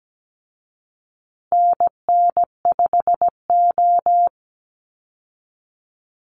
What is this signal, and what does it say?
Morse code, heard as a keyed tone, about 17 wpm, 700 Hz. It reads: NN5O
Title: Call Signs Repeated 2x - 17wpm